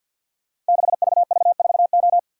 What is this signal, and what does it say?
Morse code, heard as a keyed tone, about 50 wpm, 700 Hz. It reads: 6VV4O
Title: Code Groups - 5 Letter:Numbers - 50wpm